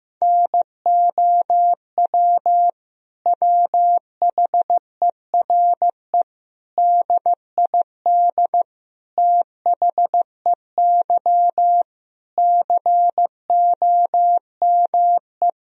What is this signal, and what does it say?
Morse code, heard as a keyed tone, about 15 wpm, 700 Hz. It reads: NOW WHERE DID THEY COME
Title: Sentences from Top 200 Words and Abbreviations 15wpm